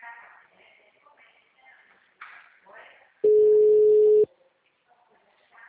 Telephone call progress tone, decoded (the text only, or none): ringback tone